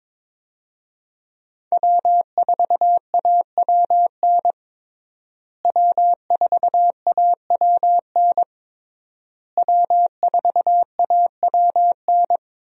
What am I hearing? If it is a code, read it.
Morse code, heard as a keyed tone, about 23 wpm, 700 Hz. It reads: W4AWN W4AWN W4AWN
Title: USA Call Signs - 5 Characters - Repeated 3x - 22wpm